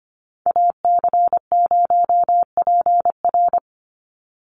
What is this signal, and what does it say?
Morse code, heard as a keyed tone, about 25 wpm, 700 Hz. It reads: AC0PR
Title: NA SOTA Activator Callsigns 25wpm